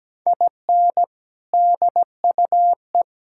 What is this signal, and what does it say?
Morse code, heard as a keyed tone, about 17 wpm, 700 Hz. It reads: IN DUE
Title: Sentences from Top 1000 Words - Review Part 6 - 17wpm